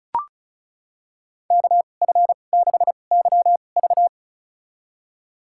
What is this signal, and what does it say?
Morse code, heard as a keyed tone, about 36 wpm, spaced as for 25 wpm, 700 Hz. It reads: KF6YV
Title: USA Call Signs - 5 Characters - 2x Character Spacing - 35wpm